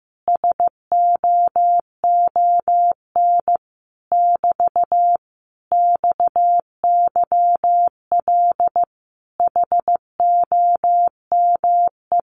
Morse code, heard as a keyed tone, about 15 wpm, 700 Hz. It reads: SOON <BT> XYL HOME
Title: Common QSO Phrases Encore 15wpm